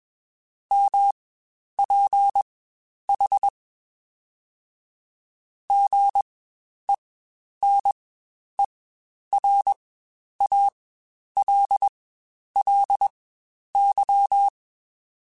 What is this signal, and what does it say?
Morse code, heard as a keyed tone, about 21 wpm, spaced as for 10 wpm, 800 Hz. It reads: MPH GENERALLY
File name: Spaced Characters / Spaced Characters-mp3s / Weather 4 10 sp @20